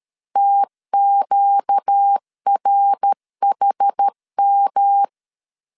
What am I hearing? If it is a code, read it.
Morse code, heard as a keyed tone, about 13 wpm, 800 Hz. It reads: TQRHM
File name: CW / DET / 1 6 1500C S B 12wpm 3431